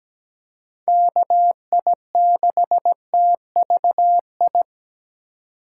Morse code, heard as a keyed tone, about 17 wpm, 700 Hz. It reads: KI6TVI
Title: USA Call Signs - 6 Characters - 3x Character Spacing - 17wpm